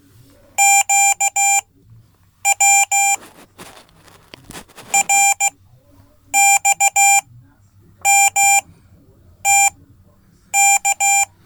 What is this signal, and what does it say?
Morse code, heard as a keyed tone, about 16 wpm, spaced as for 8 wpm, 790 Hz. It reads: QW RXMTK
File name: Code / 5min-c